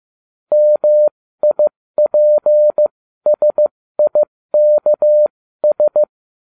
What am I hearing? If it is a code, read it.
Morse code, heard as a keyed tone, about 15 wpm, spaced as for 13 wpm, 600 Hz. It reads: MIPSIKS